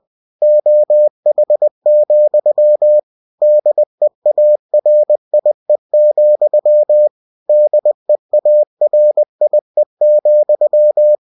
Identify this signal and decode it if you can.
Morse code, heard as a keyed tone, about 20 wpm, 600 Hz. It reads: OH, DEARIE, DEARIE,
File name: cw-books / winnie-the-pooh / winnie-20wpm-page-0053